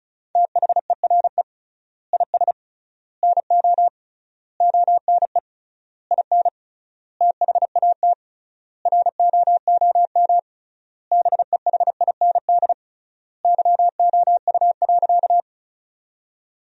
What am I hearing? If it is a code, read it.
Morse code, heard as a keyed tone, about 34 wpm, 700 Hz. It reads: THERE IS NO ONE IN THAT ROOM BEHIND YOU.
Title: 3x Word-Spacing - Sentences from Top 500 Words - 35wpm